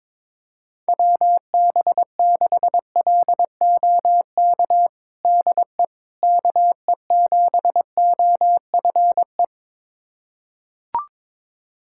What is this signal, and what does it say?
Morse code, heard as a keyed tone, about 22 wpm, 700 Hz. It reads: WB6LOK DE KE7OFE
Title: USA Callsign DE Callsign - 6 Characters - 22wpm